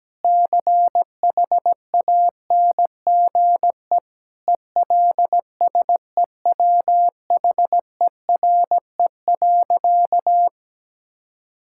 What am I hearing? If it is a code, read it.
Morse code, heard as a keyed tone, about 17 wpm, 700 Hz. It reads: CHANGE ELSEWHERE.
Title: Sentences from Top 1600 Words 17wpm